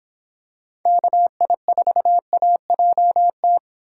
Morse code, heard as a keyed tone, about 26 wpm, 700 Hz. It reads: KI4AJT